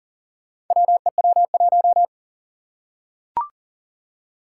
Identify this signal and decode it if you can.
Morse code, heard as a keyed tone, about 40 wpm, 700 Hz. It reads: WEW1
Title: Four Challenging Letter-Number Combinations 40wpm